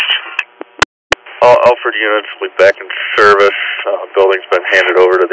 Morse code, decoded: none

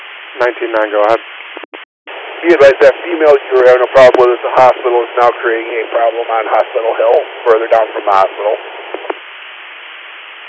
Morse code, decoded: none

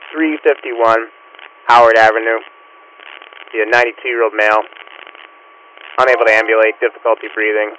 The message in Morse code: none

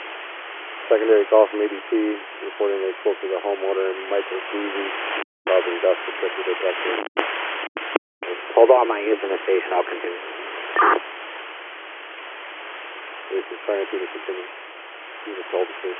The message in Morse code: none

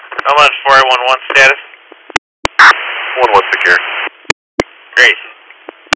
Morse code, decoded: none

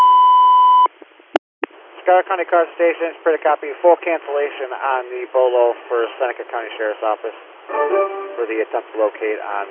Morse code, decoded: none

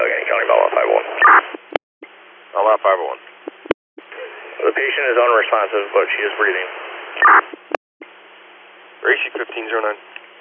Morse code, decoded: none

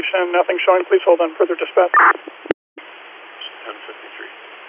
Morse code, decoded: none

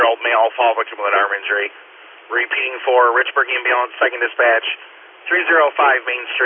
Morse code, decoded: none

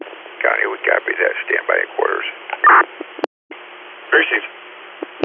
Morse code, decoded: none

